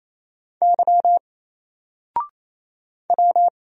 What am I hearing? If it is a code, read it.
Morse code, heard as a keyed tone, about 27 wpm, 700 Hz. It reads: Y W
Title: Single Character (Letter:Number:Punctuation:Prosign) 28wpm